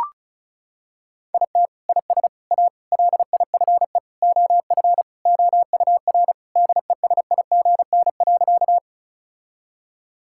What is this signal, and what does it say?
Morse code, heard as a keyed tone, about 36 wpm, 700 Hz. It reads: IT IS A LIFE OF OUR DESIGN.
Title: Sentences from Top 1000 Words - Review Part 5 - 35wpm